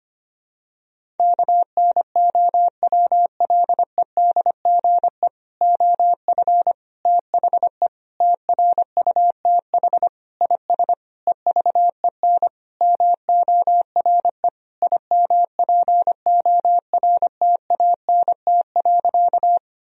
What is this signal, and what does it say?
Morse code, heard as a keyed tone, about 25 wpm, 700 Hz. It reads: KNOWLEDGE OF THE TRUTH IS EVEN MORE IMPORTANT.
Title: Sentences from Top 1000 Words - Review Part 3 - 25wpm